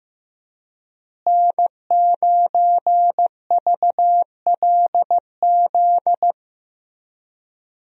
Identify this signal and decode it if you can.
Morse code, heard as a keyed tone, about 15 wpm, 700 Hz. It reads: N9VLZ
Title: USA Call Signs - 5 Characters - Repeated 2x - 15wpm